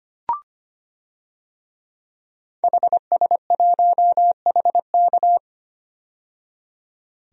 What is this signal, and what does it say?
Morse code, heard as a keyed tone, about 25 wpm, 700 Hz. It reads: HS1HK
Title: DXCC-100 Call Signs - 5 Characters - 25wpm